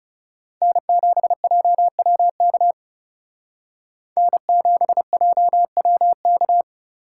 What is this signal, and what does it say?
Morse code, heard as a keyed tone, about 38 wpm, 700 Hz. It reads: N7JWK N7JWK